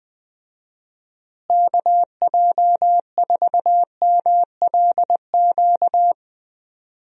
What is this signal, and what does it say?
Morse code, heard as a keyed tone, about 20 wpm, 700 Hz. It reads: KJ4MLQ
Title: USA Call Signs - 6 Characters - Repeated 3x - 20wpm